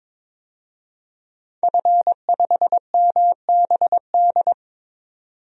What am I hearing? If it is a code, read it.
Morse code, heard as a keyed tone, about 22 wpm, 700 Hz. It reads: F5MBD
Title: DXCC-100 Call Signs - 5 Characters - 22wpm